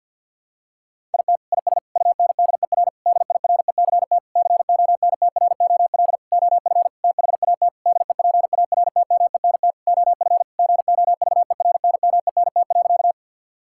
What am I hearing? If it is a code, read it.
Morse code, heard as a keyed tone, about 52 wpm, 700 Hz. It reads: IT IS UNDER DIRECT CONTROL OF THAT DEPARTMENT OF GOVERNMENT.
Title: Sentences from Top 1000 Words - Review Part 5 - 50wpm